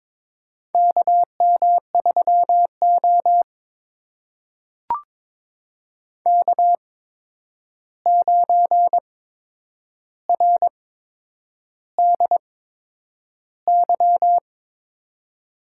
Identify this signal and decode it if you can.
Morse code, heard as a keyed tone, about 22 wpm, spaced as for 9 wpm, 700 Hz. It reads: KM3O K9RDY
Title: USA Call Signs - 8x Character Spacing - 22wpm